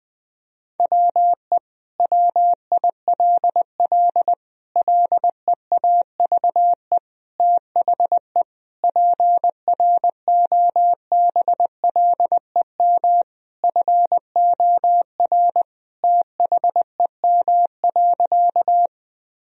Morse code, heard as a keyed tone, about 20 wpm, 700 Hz. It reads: WE WILL LEAVE THE PROBLEM FOR THEM.
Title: Sentences from Top 1000 Words - Review Part 2 - 20wpm